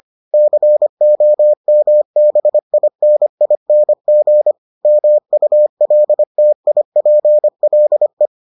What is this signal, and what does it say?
Morse code, heard as a keyed tone, about 25 wpm, 600 Hz. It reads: COMBINING MULTIPLE